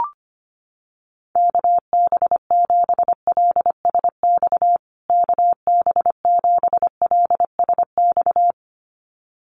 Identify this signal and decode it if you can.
Morse code, heard as a keyed tone, about 25 wpm, 700 Hz. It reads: KB7LSX KB7LSX
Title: USA Call Signs Repeated 2x 25wpm